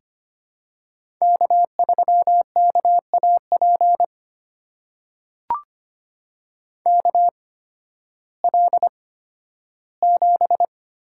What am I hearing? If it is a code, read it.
Morse code, heard as a keyed tone, about 25 wpm, spaced as for 13 wpm, 700 Hz. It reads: K3KAP KL7
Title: USA Call Signs - 5 Characters - 8x Character Spacing - 25wpm